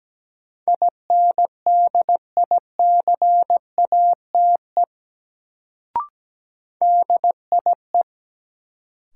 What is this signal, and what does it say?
Morse code, heard as a keyed tone, about 17 wpm, 700 Hz. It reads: INDICATE DIE
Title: Sentences from Top 1300 Words - New Word Drill - 17wpm